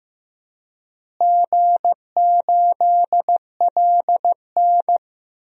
Morse code, heard as a keyed tone, about 15 wpm, 700 Hz. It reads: G8LN
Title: DXCC-100 Call Signs - 4 Characters - Repeated 2x - 15wpm